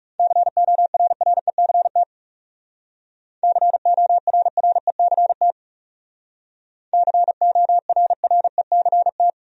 Morse code, heard as a keyed tone, about 39 wpm, 700 Hz. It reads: CORRECT CORRECT CORRECT